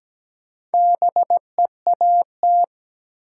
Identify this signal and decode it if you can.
Morse code, heard as a keyed tone, about 17 wpm, 700 Hz. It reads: BEAT
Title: Sentences from Top 1300 Words - New Word Drill Alt - 17wpm